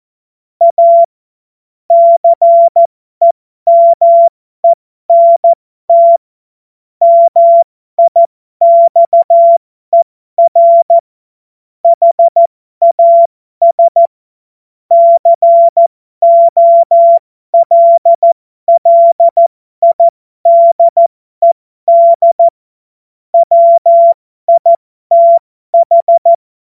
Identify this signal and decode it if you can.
Morse code, heard as a keyed tone, about 14 wpm, spaced as for 12 wpm, 680 Hz. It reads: A CEMENT MIXER HAS COLLIDED WITH
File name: Morsekurs / Witze